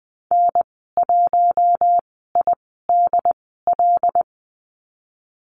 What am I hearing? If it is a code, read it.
Morse code, heard as a keyed tone, about 20 wpm, spaced as for 14 wpm, 700 Hz. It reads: N1IDL
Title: 076 Lesson 13g - Letter L - Call Signs 20wpm